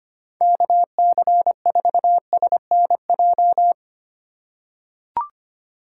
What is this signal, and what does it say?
Morse code, heard as a keyed tone, about 25 wpm, 700 Hz. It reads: KC4SNJ
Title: USA Call Signs - 6 Characters - 3x Character Spacing - 25wpm